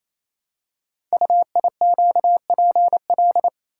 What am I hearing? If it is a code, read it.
Morse code, heard as a keyed tone, about 28 wpm, 700 Hz. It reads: UIQPL
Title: Code Groups - 5 Letters - 28wpm